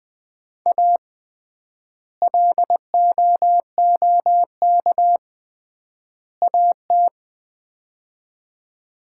Morse code, heard as a keyed tone, about 20 wpm, 700 Hz. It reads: A LOOK AT
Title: 3x Word-Spacing - Sets of 3 Words - 20wpm